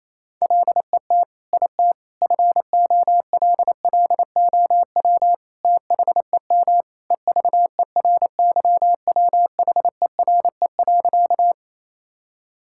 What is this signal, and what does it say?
Morse code, heard as a keyed tone, about 28 wpm, 700 Hz. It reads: LET IT FOLLOW THEM EVERYWHERE.
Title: Sentences from Top 1600 Words 28wpm